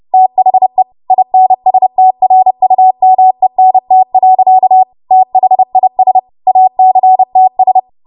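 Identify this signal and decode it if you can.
Morse code, heard as a keyed tone, about 30 wpm, 750 Hz. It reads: THE INSTRUMENT. THIS ACTS
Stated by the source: W1AW Web Code Practice Files, 30 WPM run from ARRL; text from April 2023 QST.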